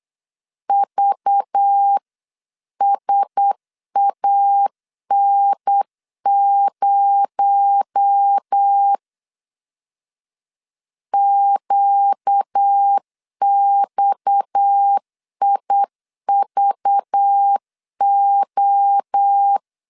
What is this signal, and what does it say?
Morse code, heard as a keyed tone, about 8 wpm, 800 Hz. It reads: VSAN0 QXIVO
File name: CW / DET / 1 6 1500C S B 8wpm 5142